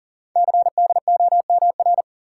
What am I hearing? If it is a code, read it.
Morse code, heard as a keyed tone, about 39 wpm, 700 Hz. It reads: CDOMR
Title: Code Groups - 5 Letters - 40wpm